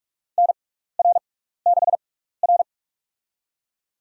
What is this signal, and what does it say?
Morse code, heard as a keyed tone, about 46 wpm, spaced as for 15 wpm, 700 Hz. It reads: NR6R
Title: USA Call Signs - 4 Characters - 6x Character Spacing - 45wpm